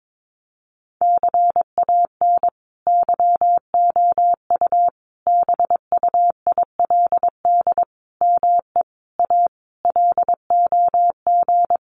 Morse code, heard as a keyed tone, about 22 wpm, 700 Hz. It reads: CAN YOU BUILD ME A LOG
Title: Sentences from Top 1400 Words 22wpm